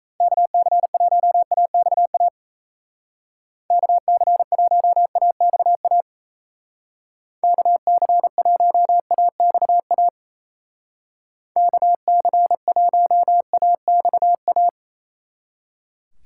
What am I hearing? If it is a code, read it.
Morse code, heard as a keyed tone, about 37 wpm, 700 Hz. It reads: KC1AXA KC1AXA KC1AXA KC1AXA